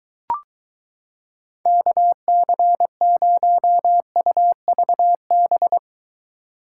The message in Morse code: KC0UVB

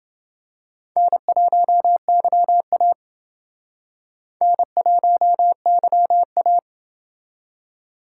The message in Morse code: N1YA N1YA

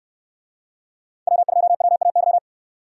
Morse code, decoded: W2WAY

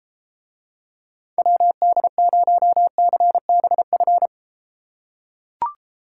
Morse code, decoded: WD0CBF

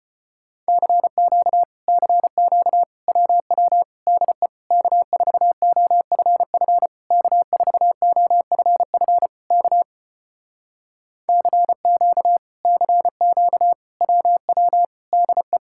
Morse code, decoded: CQ CQ WW DE K4OFF K4OFF K CQ CQ WW DE

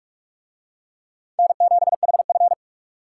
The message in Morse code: N7HF

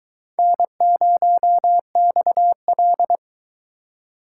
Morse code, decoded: N0XL

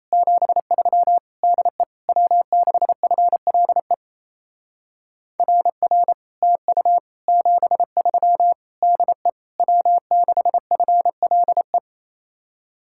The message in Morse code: 73 DE W6FLE RR TU 73 DE W6FLE

